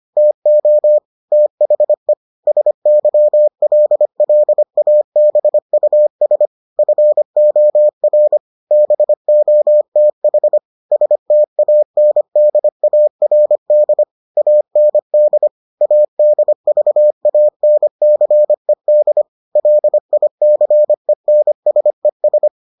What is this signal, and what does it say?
Morse code, heard as a keyed tone, about 25 wpm, 600 Hz. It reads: TO THE SYLLABUS FOR BOTH STANDARD AND ADVANCED LICENSES